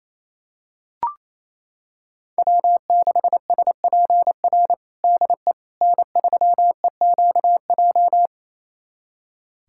Morse code, heard as a keyed tone, about 28 wpm, 700 Hz. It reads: W6SPR DE N3EQJ
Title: USA Callsign DE Callsign - 5 Characters - 28wpm